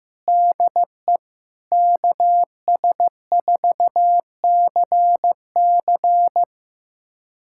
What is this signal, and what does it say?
Morse code, heard as a keyed tone, about 15 wpm, 700 Hz. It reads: DE KS4CC
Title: USA Callsign DE Callsign 15wpm